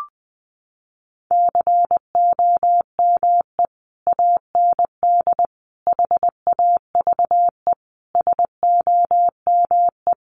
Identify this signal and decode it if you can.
Morse code, heard as a keyed tone, about 20 wpm, 700 Hz. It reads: COME AND HAVE SOME
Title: Sentences from Top 1500 Words 20wpm